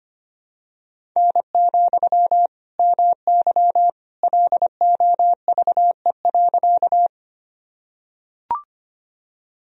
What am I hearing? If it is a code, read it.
Morse code, heard as a keyed tone, about 25 wpm, 700 Hz. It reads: N, MY LOVE.
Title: Sentences from Top 400 Words and Abbreviations 25wpm